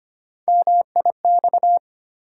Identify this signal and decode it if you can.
Morse code, heard as a keyed tone, about 25 wpm, 700 Hz. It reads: MIX
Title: Sentences from Top 1100 Words - New Word Drill Alt - 25wpm